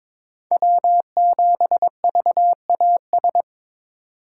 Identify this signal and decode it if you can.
Morse code, heard as a keyed tone, about 22 wpm, 700 Hz. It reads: W7VAS
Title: K1USN SST QSO Practice - 22wpm